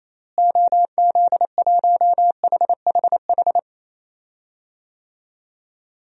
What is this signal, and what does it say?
Morse code, heard as a keyed tone, about 28 wpm, 700 Hz. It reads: OZ1HHH
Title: DXCC-100 Call Signs - 6 Characters - Repeated 2x - 28wpm